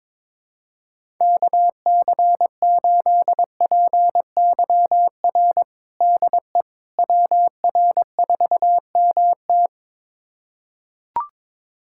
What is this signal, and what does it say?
Morse code, heard as a keyed tone, about 22 wpm, 700 Hz. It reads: KC8PYR DE WR4MT
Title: USA Callsign DE Callsign 22wpm